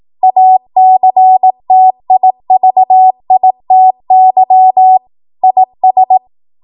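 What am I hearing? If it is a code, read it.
Morse code, heard as a keyed tone, about 18 wpm, 750 Hz. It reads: ACTIVITY IS
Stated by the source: ARRL 18 wpm Morse code practice archive transmission.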